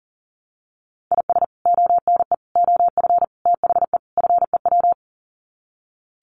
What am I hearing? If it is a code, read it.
Morse code, heard as a keyed tone, about 40 wpm, 700 Hz. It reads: IS ONE OF THE FEW
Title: Sets of 5 Words 40wpm